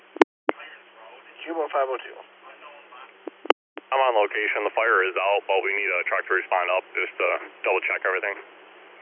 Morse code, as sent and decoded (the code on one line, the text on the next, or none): none